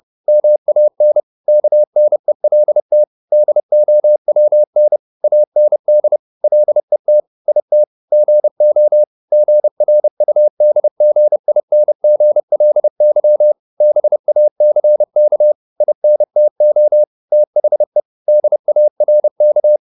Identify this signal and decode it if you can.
Morse code, heard as a keyed tone, about 30 wpm, 600 Hz. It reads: MAN KNELT DOWN AND LET IT GO GRUDGINGLY BACK INTO THE DARK